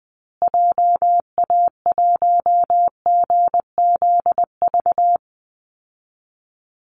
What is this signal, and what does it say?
Morse code, heard as a keyed tone, about 20 wpm, 700 Hz. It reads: JA1GZV
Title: Call Signs Repeated 2x - 20wpm